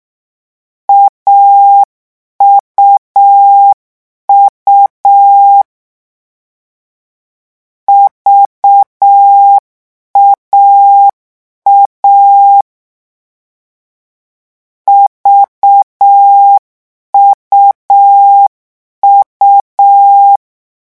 Morse code, heard as a keyed tone, about 6 wpm, 800 Hz. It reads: AUU VAA VUU
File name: morse mp3 / lecon3-vitesse 6